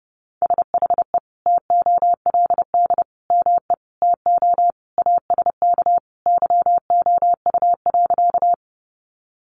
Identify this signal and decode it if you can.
Morse code, heard as a keyed tone, about 30 wpm, 700 Hz. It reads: SHE TOLD ME TO ASK YOU.
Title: Sentences from Top 400 Words 30wpm